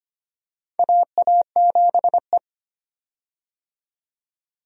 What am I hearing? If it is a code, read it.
Morse code, heard as a keyed tone, about 25 wpm, 700 Hz. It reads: AA7E
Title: USA Call Signs - 4 Characters - 25wpm